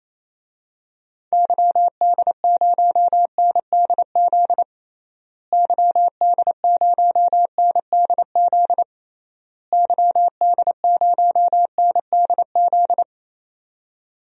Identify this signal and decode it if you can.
Morse code, heard as a keyed tone, about 28 wpm, 700 Hz. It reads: YD0NDZ YD0NDZ YD0NDZ
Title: DXCC-100 Call Signs - 6 Characters - Repeated 3x - 28wpm